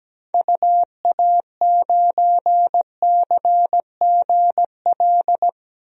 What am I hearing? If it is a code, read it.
Morse code, heard as a keyed tone, about 17 wpm, 700 Hz. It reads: UA9CGL
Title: DXCC-100 Call Signs - 6 Characters - Repeated 3x - 17wpm